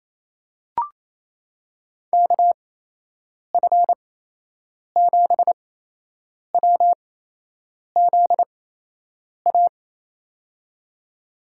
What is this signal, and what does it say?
Morse code, heard as a keyed tone, about 28 wpm, spaced as for 8 wpm, 700 Hz. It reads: KF7WZA
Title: USA Call Signs - 6 Characters - 8x Character Spacing - 28wpm